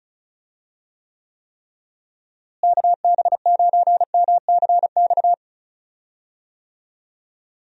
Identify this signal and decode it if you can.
Morse code, heard as a keyed tone, about 36 wpm, 700 Hz. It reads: KB9MCX
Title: USA Call Signs - 6 Characters - Repeated 3x - 35wpm